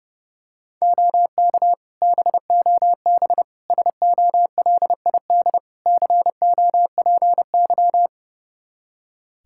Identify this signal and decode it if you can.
Morse code, heard as a keyed tone, about 30 wpm, 700 Hz. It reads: OK BOB SOLID COPY
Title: Practice QSO - Exchange 3 - More Info - 30wpm